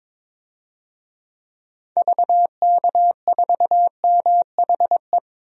Morse code, heard as a keyed tone, about 22 wpm, 700 Hz. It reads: VK4MHE
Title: DXCC-100 Call Signs - 6 Characters - 22wpm